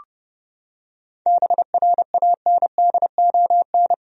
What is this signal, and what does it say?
Morse code, heard as a keyed tone, about 30 wpm, 700 Hz. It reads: BRANDON
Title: USA Common Names - 30wpm